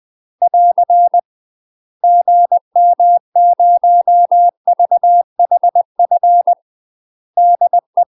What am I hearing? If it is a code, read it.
Morse code, heard as a keyed tone, about 20 wpm, 700 Hz. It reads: <AR> GM0VHF DE
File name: UK MASTER QSOs TRAINING / 20WPM / UK MASTER QSOs 0062